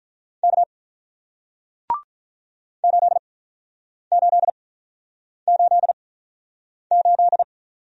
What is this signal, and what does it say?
Morse code, heard as a keyed tone, about 46 wpm, spaced as for 9 wpm, 700 Hz. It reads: K 8888